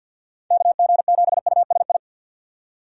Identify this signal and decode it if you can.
Morse code, heard as a keyed tone, about 46 wpm, 700 Hz. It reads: KG7USI
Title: USA Call Signs - 6 Characters - 50wpm